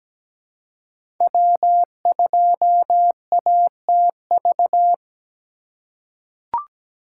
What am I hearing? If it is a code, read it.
Morse code, heard as a keyed tone, about 17 wpm, 700 Hz. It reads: W2ATV
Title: USA Call Signs - 5 Characters - 4x Character Spacing - 17wpm